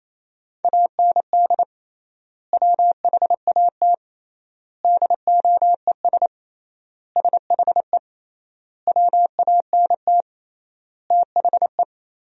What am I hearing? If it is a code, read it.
Morse code, heard as a keyed tone, about 28 wpm, 700 Hz. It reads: AND WHAT DOES SHE WANT THE
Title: 3x Word-Spacing - Sentences from Top 400 Words - 28wpm